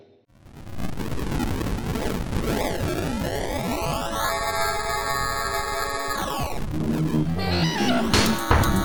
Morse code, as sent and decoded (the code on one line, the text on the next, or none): none